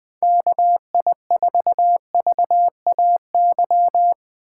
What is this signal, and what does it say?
Morse code, heard as a keyed tone, about 20 wpm, 700 Hz. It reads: KI4VAY
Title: USA Call Signs - 6 Characters - 4x Character Spacing - 20wpm